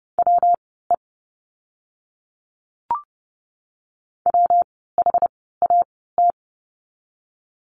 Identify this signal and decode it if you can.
Morse code, heard as a keyed tone, about 30 wpm, spaced as for 17 wpm, 700 Hz. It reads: WE WHAT
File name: course / 30:10wpm / 115 Lesson 19f - Letter w - Words 30wpm